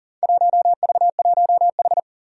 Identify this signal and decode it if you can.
Morse code, heard as a keyed tone, about 38 wpm, 700 Hz. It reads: 1V1H
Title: Four Challenging Letter-Number Combinations 40wpm